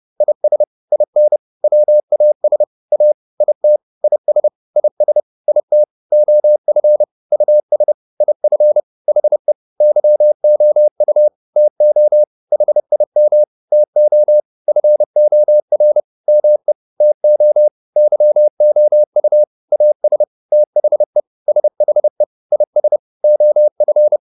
Morse code, heard as a keyed tone, about 30 wpm, 600 Hz. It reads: IS IN WAS A IT IS IS IT OF US IF HE YOU TO HIM TO FOR ME TO YOU AS THE SHE IS OF